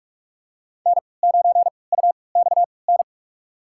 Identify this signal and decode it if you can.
Morse code, heard as a keyed tone, about 47 wpm, spaced as for 26 wpm, 700 Hz. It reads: N9UXN
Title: USA Call Signs - 5 Characters - 3x Character Spacing - 45wpm